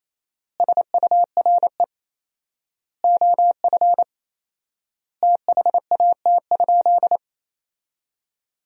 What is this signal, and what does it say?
Morse code, heard as a keyed tone, about 28 wpm, 700 Hz. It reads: SURE OF THAT?
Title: 4x Word-Spacing - Sentences from Top 300 Words - 28wpm